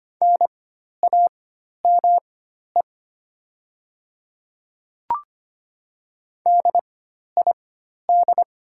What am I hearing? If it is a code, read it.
Morse code, heard as a keyed tone, about 25 wpm, spaced as for 12 wpm, 690 Hz. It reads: NAME DID
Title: 4x Character Spacing - Top 200 Words - 25wpm